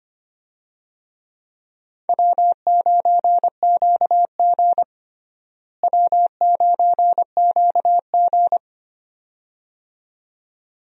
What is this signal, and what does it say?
Morse code, heard as a keyed tone, about 25 wpm, 700 Hz. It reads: W9QG W9QG
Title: USA Call Signs - 4 Characters - Repeated 2x - 25wpm